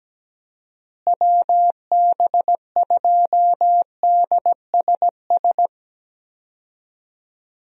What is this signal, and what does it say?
Morse code, heard as a keyed tone, about 17 wpm, 700 Hz. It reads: WB2DSS
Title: USA Call Signs - 6 Characters - Repeated 2x - 17wpm